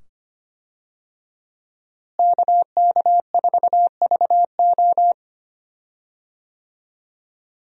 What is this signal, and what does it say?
Morse code, heard as a keyed tone, about 25 wpm, 700 Hz. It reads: KK4VO
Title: USA Call Signs - 5 Characters - Repeated 2x - 25wpm